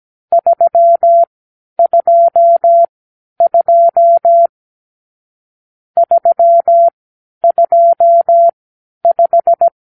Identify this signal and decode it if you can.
Morse code, heard as a keyed tone, about 17 wpm, spaced as for 10 wpm, 680 Hz. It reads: 322 325
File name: arrl / Review (1-5)